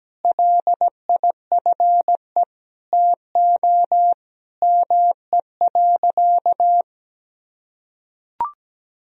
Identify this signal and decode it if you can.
Morse code, heard as a keyed tone, about 17 wpm, 700 Hz. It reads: LIFE TO ME.
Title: Sentences from Top 500 Words and Abbreviations 17wpm